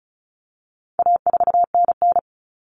Morse code, heard as a keyed tone, about 34 wpm, 700 Hz. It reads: A4NN
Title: Common Cut Numbers - 4-Digit 35wpm